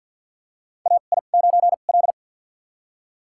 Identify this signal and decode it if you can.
Morse code, heard as a keyed tone, about 52 wpm, spaced as for 36 wpm, 700 Hz. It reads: AI9L
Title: USA Call Signs - 2x Character Spacing - 50wpm